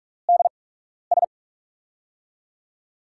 Morse code, heard as a keyed tone, about 45 wpm, spaced as for 12 wpm, 700 Hz. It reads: DS
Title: USA Call Signs - 8x Character Spacing - 45wpm